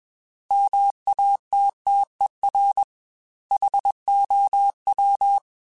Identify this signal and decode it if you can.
Morse code, heard as a keyed tone, about 21 wpm, 800 Hz. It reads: MATTER HOW
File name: Technical Subjects / Technical Subjects-mp3s / OR Gate @20